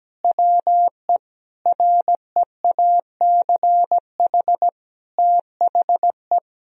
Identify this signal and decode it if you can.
Morse code, heard as a keyed tone, about 17 wpm, 700 Hz. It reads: WE REACH THE
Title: Sentences from Top 1000 Words - Review Part 4 - 17wpm